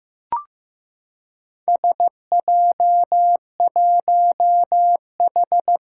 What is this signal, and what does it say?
Morse code, heard as a keyed tone, about 15 wpm, 700 Hz. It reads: SJ1H
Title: Four Challenging Letter-Number Combinations 15wpm